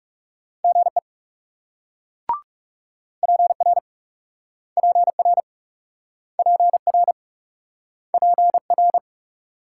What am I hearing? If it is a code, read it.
Morse code, heard as a keyed tone, about 38 wpm, 700 Hz. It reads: GE PR PR PR PR